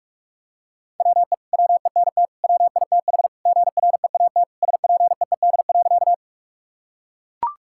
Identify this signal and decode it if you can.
Morse code, heard as a keyed tone, about 41 wpm, 700 Hz. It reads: WE WENT WITH GREAT SPEED.
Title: Sentences from Top 1000 Words - Review Part 5 - 45wpm